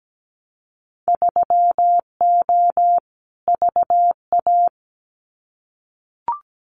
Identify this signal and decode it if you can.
Morse code, heard as a keyed tone, about 17 wpm, 700 Hz. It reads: 3O VA
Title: Winter Field Day Exchanges 17wpm